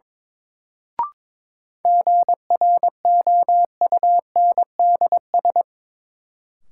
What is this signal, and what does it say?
Morse code, heard as a keyed tone, about 21 wpm, 700 Hz. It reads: GROUNDS
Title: Sentences from Top 1300 Words - New Word Drill Alt - 22wpm